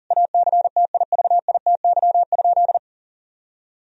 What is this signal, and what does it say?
Morse code, heard as a keyed tone, about 37 wpm, 700 Hz. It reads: ACTIVITY?
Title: Sentences from Top 1000 Words - Review Part 4 - 40wpm